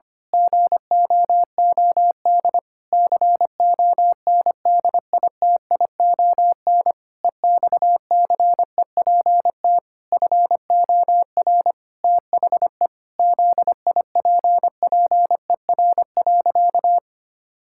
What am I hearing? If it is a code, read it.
Morse code, heard as a keyed tone, about 25 wpm, 700 Hz. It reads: GOOD CONDITION EXCEPT FOR THE ZIPPER.
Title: Pangram Challenge v2 - 25wpm